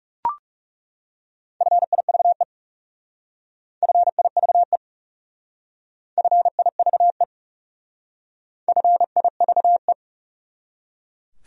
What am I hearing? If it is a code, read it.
Morse code, heard as a keyed tone, about 36 wpm, 700 Hz. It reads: FIVE FIVE FIVE FIVE
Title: Speed-Racing Top 500 Words 30wpm